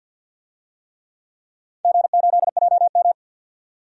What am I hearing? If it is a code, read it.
Morse code, heard as a keyed tone, about 51 wpm, 700 Hz. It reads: G8JM
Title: DXCC-100 Call Signs - 4 Characters - 50wpm